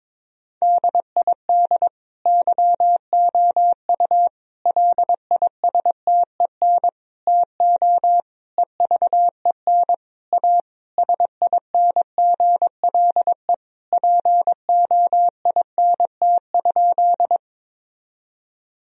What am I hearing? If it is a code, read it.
Morse code, heard as a keyed tone, about 22 wpm, 700 Hz. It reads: DID YOU LISTEN TO EVEN A SINGLE POINT?
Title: Sentences from Top 1000 Words - Review Part 4 - 22wpm